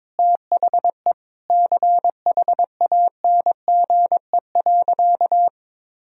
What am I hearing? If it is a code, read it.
Morse code, heard as a keyed tone, about 22 wpm, 700 Hz. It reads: THE CHANGE.
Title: Sentences from Top 1000 Words - Review Part 1 - 22wpm